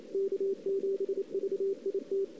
Morse code, heard as a keyed tone, about 29 wpm, 410 Hz. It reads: K7VIT
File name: CW / 2008-ss-cw-k5tr-op-n5rz -02:00